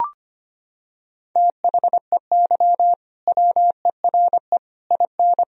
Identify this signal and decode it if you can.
Morse code, heard as a keyed tone, about 25 wpm, 700 Hz. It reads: THEY WERE IN
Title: Sentences from Top 1000 Words - Review Part 3 - 25wpm